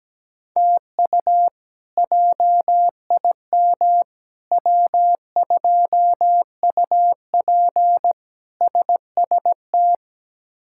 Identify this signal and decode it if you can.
Morse code, heard as a keyed tone, about 17 wpm, 700 Hz. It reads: TU JIM W2UP SST
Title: K1USN SST QSO Practice - 17wpm